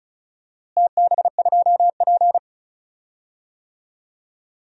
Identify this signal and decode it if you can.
Morse code, heard as a keyed tone, about 36 wpm, 700 Hz. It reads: TB2P
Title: DXCC-100 Call Signs - 4 Characters - Repeated 2x - 35wpm